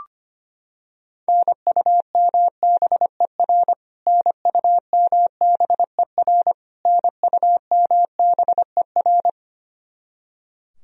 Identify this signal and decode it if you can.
Morse code, heard as a keyed tone, about 25 wpm, 700 Hz. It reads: NUMBER NUMBER NUMBER
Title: Top 100 Words Repeated 25wpm